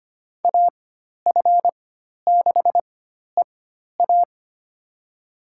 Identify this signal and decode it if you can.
Morse code, heard as a keyed tone, about 25 wpm, spaced as for 12 wpm, 700 Hz. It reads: AF6EA
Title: USA Call Signs - 5 Characters - 4x Character Spacing - 25wpm